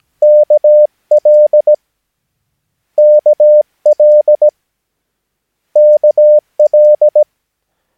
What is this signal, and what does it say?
Morse code, heard as a keyed tone, about 17 wpm, 600 Hz. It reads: KL KL KL
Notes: Övningsmaterial med alfabetet A till Ö samt siffror 0 till 9 enligt International/Continental Morse Code.